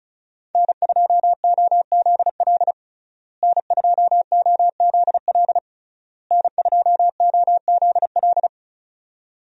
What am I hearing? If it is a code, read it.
Morse code, heard as a keyed tone, about 36 wpm, 700 Hz. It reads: N2OZL N2OZL N2OZL
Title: USA Call Signs - 5 Characters - Repeated 3x - 35wpm